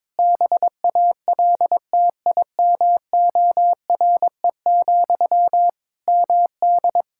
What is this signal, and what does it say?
Morse code, heard as a keyed tone, about 21 wpm, 700 Hz. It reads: BALTIMORE, MD
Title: Practice QSO - Exchange 1 - Beginning - 22wpm